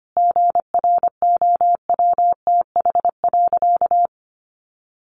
Morse code, heard as a keyed tone, about 25 wpm, 700 Hz. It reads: GROWTH.
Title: Sentences from Top 1300 Words 25wpm